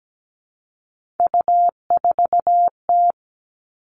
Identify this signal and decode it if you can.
Morse code, heard as a keyed tone, about 17 wpm, 700 Hz. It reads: U4T
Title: All Cut Numbers - 3-Digit 17wpm